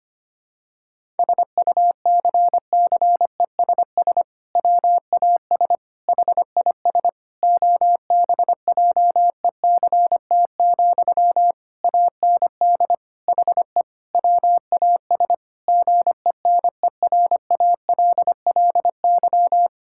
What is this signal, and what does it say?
Morse code, heard as a keyed tone, about 25 wpm, 700 Hz. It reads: SUCCESS WAS HIS OBJECT, AND HE WAS GENERALLY
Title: Sentences from Top 1000 Words - Review Part 6 - 25wpm